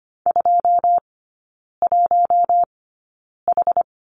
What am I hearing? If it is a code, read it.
Morse code, heard as a keyed tone, about 25 wpm, spaced as for 9 wpm, 690 Hz. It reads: 21H
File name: Lesson 14e - Number 2 - Three Characters 25wpm